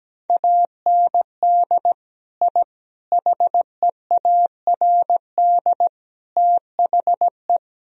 Morse code, heard as a keyed tone, about 17 wpm, 700 Hz. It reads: AND I HEARD THE
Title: Sentences from Top 1000 Words - Review Part 1 - 17wpm